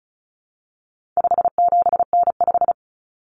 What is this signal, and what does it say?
Morse code, heard as a keyed tone, about 34 wpm, 700 Hz. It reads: 57N5
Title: Common Cut Numbers - 4-Digit 35wpm